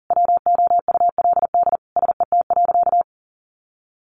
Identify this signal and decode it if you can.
Morse code, heard as a keyed tone, about 40 wpm, 700 Hz. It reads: WOULD SET.
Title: Sentences from Top 400 Words 40wpm